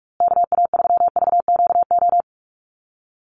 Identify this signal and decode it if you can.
Morse code, heard as a keyed tone, about 44 wpm, 700 Hz. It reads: KA3VQO
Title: USA Call Signs Repeated 2x 45wpm